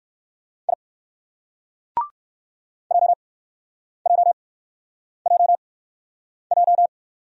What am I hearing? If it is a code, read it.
Morse code, heard as a keyed tone, about 56 wpm, spaced as for 9 wpm, 700 Hz. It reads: I JJJJ
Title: Speed-Racing Single Letter-Number 45wpm